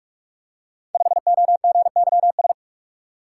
Morse code, heard as a keyed tone, about 44 wpm, 700 Hz. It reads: 5OGYS
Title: Code Groups - 5 Letter:Numbers - 45wpm